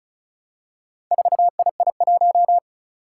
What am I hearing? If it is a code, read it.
Morse code, heard as a keyed tone, about 36 wpm, 700 Hz. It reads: 4II1
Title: Four Challenging Letter-Number Combinations 35wpm